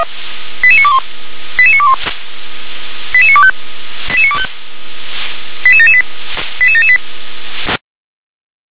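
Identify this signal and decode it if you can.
Five-tone selective call (ZVEI), heard as quick runs of five tones, tones of about 70 ms. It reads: 88921 88921 88925 88925 88778 88778
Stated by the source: ZVEI five-tone call received on SDR.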